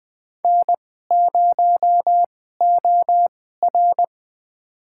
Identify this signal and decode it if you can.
Morse code, heard as a keyed tone, about 20 wpm, spaced as for 14 wpm, 700 Hz. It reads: N0OR
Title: USA Call Signs - 2x Character Spacing - 20wpm